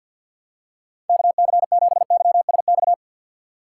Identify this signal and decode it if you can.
Morse code, heard as a keyed tone, about 49 wpm, 700 Hz. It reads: KC7YSX
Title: USA Call Signs - 6 Characters - 4x Character Spacing - 50wpm